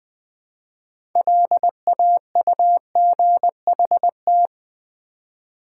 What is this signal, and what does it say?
Morse code, heard as a keyed tone, about 20 wpm, 700 Hz. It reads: LAUGHT
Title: Build A Word - 3 Letters At A Time - 8-Letter Words - 20wpm